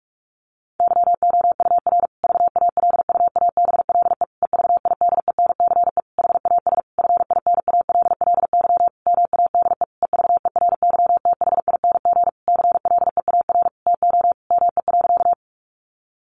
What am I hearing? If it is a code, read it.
Morse code, heard as a keyed tone, about 44 wpm, 700 Hz. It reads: YOUR VALUABLE EVIDENCE HAS FINALLY MADE EVERYTHING CLEAR TO ME.
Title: Sentences from Top 1300 Words 45wpm